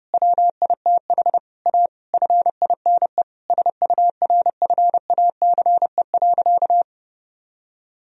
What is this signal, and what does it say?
Morse code, heard as a keyed tone, about 30 wpm, 700 Hz. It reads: WITH A FINE SURFACE.
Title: Sentences from Top 1000 Words - Review Part 3 - 30wpm